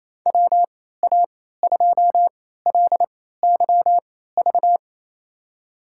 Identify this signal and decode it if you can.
Morse code, heard as a keyed tone, about 28 wpm, spaced as for 16 wpm, 700 Hz. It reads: WA2LYV
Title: USA Call Signs - 6 Characters - 3x Character Spacing - 28wpm